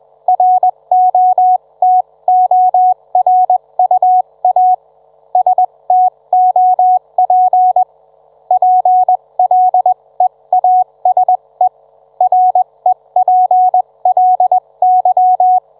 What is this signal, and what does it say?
Morse code, heard as a keyed tone, about 21 wpm, spaced as for 18 wpm, 740 Hz. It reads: ROTORUA STOP PLEASE REPLY